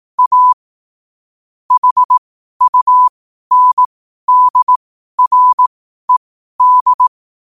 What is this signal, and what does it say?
Morse code, heard as a keyed tone, about 18 wpm, spaced as for 12 wpm, 1000 Hz. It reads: A HUNDRED
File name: qotdfiles / 12wpm